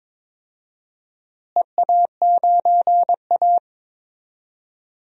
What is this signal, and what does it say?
Morse code, heard as a keyed tone, about 22 wpm, 700 Hz. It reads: EA9A
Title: DXCC-100 Call Signs - 4 Characters - Repeated 2x - 22wpm